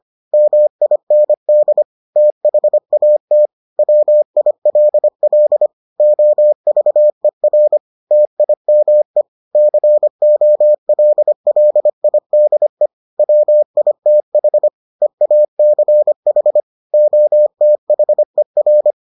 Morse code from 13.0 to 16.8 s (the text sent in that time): WITH EACH